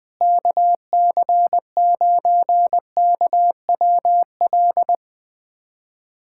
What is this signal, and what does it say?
Morse code, heard as a keyed tone, about 19 wpm, 700 Hz. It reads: KC9KWL
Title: USA Callsign DE Callsign - 6 Characters - 20wpm